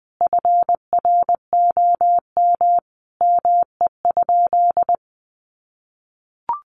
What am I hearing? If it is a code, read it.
Morse code, heard as a keyed tone, about 20 wpm, 700 Hz. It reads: FROM ME?
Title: Sentences from Top 1500 Words 20wpm